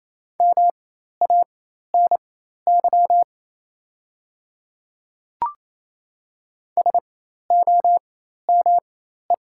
Morse code, heard as a keyed tone, about 27 wpm, spaced as for 13 wpm, 700 Hz. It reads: MANY SOME
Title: 4x Character Spacing - Top 100 Words - 28wpm